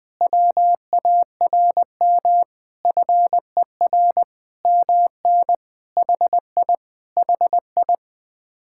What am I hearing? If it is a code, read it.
Morse code, heard as a keyed tone, about 20 wpm, 700 Hz. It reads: WARM FER MN HI HI
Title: Speed-Racing Common QSO Phrases - 17wpm